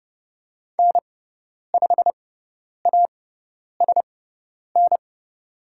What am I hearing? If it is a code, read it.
Morse code, heard as a keyed tone, about 30 wpm, spaced as for 10 wpm, 700 Hz. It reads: N5ASN
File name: Lesson 16g - Letter c - Call Signs 30wpm